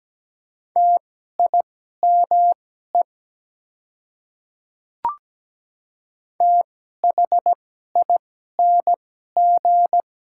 Morse code, heard as a keyed tone, about 17 wpm, spaced as for 12 wpm, 700 Hz. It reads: TIME THING
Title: 2x Character Spacing - Top 200 Words - 17wpm